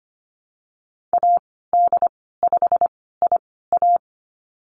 Morse code, heard as a keyed tone, about 25 wpm, spaced as for 16 wpm, 700 Hz. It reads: AD5IA
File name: Lesson 15g - Number 5 - Call Signs 25wpm